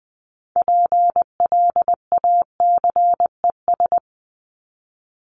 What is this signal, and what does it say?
Morse code, heard as a keyed tone, about 20 wpm, 700 Hz. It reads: PLACES
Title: ES:S Suffix - Top 100 Words with Base Words 20wpm